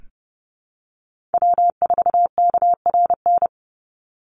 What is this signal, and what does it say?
Morse code, heard as a keyed tone, about 30 wpm, 700 Hz. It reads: W4KRN
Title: NA SOTA Chaser Callsigns 30wpm